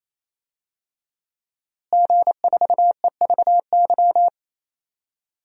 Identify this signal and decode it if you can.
Morse code, heard as a keyed tone, about 28 wpm, 700 Hz. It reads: G4EVY
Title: DXCC-100 Call Signs - 5 Characters - Repeated 3x - 28wpm